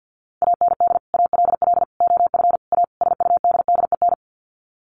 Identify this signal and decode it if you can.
Morse code, heard as a keyed tone, about 50 wpm, 700 Hz. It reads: AND ALL OF A SUDDEN
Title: Sets of 5 Words 50wpm